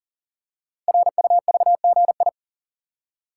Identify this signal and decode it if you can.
Morse code, heard as a keyed tone, about 42 wpm, 690 Hz. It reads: RUVGI
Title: Code Groups - 5 Letter:Numbers - 40wpm